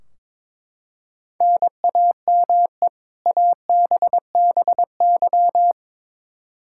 Morse code, heard as a keyed tone, about 22 wpm, 700 Hz. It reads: NAME ABBY
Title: Practice QSO - Exchange 1 - Beginning - 22wpm